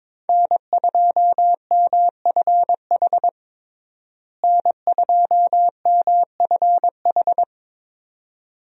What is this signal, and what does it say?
Morse code, heard as a keyed tone, about 22 wpm, 700 Hz. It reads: N2MFH N2MFH
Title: USA Call Signs - 5 Characters - Repeated 2x - 22wpm